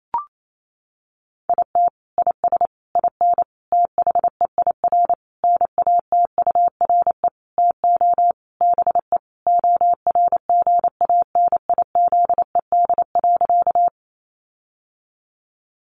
Morse code, heard as a keyed tone, about 28 wpm, 700 Hz. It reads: IT IS IN THEIR NATURE TO BE ORGANIZED.
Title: Sentences from Top 1300 Words 28wpm